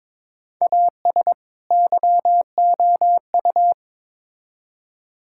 Morse code, heard as a keyed tone, about 22 wpm, 700 Hz. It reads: AS YOU
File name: Build A Phrase - 2-Word Phrases - 22wpm